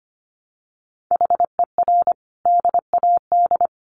Morse code, heard as a keyed tone, about 25 wpm, 700 Hz. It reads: HER DAD
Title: Additional Easy Phrases 25wpm